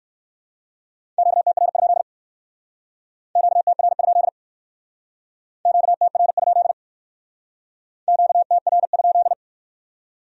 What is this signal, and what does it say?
Morse code, heard as a keyed tone, about 58 wpm, 700 Hz. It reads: QTR? QTR? QTR? QTR?